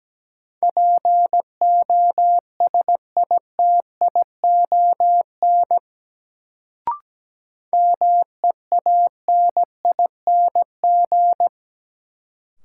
Morse code, heard as a keyed tone, about 17 wpm, 700 Hz. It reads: POSITION MEANING
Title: Sentences from Top 900 Words - New Word Drill - 17wpm